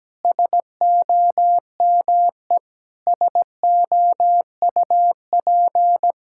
Morse code, heard as a keyed tone, about 17 wpm, 700 Hz. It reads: SOME SOUP
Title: Sentences from Top 1000 Words - Review Part 6 - 17wpm